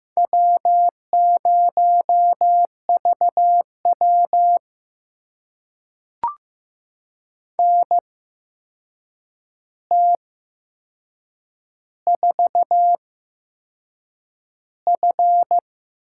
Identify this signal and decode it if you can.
Morse code, heard as a keyed tone, about 15 wpm, spaced as for 6 wpm, 700 Hz. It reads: W0VW NT4F
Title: USA Call Signs - 4 Characters - 8x Character Spacing - 15wpm